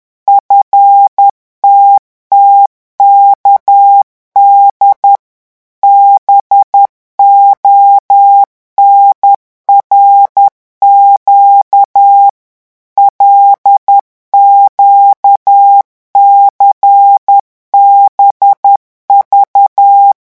Morse code, heard as a keyed tone, about 10 wpm, 800 Hz. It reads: FTTKD BONRQ LQCBV
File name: morse mp3 / revision8-vitesse 10 mots